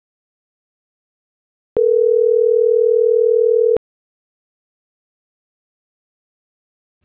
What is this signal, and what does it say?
Ringback tone (North America), 440 + 480 Hz.